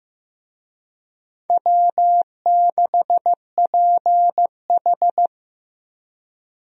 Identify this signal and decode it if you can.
Morse code, heard as a keyed tone, about 15 wpm, 700 Hz. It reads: W6PH
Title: USA Call Signs - 4 Characters - 15wpm